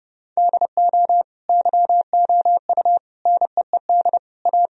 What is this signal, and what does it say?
Morse code, heard as a keyed tone, about 30 wpm, 700 Hz. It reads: DO YOU NEED A
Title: Sentences from Top 1000 Words - Review Part 2 - 30wpm